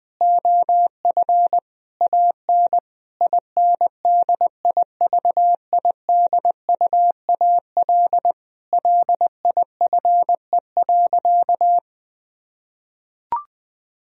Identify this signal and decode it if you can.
Morse code, heard as a keyed tone, about 20 wpm, 700 Hz. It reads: OF AN INDIVIDUAL LIFE.
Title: Sentences from Top 1000 Words - Review Part 4 - 20wpm